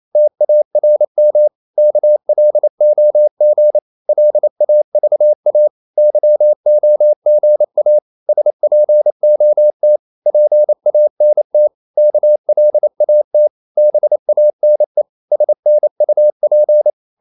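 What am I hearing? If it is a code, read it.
Morse code, heard as a keyed tone, about 28 wpm, 600 Hz. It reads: TARM KLOG LAVA YOGA SPOT PANT KLAT BANE SNUP
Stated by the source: Danish short words sent in Morse at 28 wpm, from CW med Gnister.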